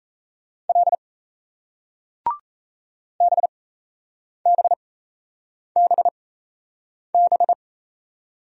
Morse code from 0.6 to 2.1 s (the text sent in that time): L